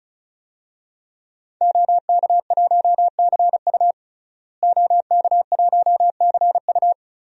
Morse code, heard as a keyed tone, about 36 wpm, 700 Hz. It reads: OK1CU OK1CU
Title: DXCC-100 Call Signs - 5 Characters - Repeated 2x - 35wpm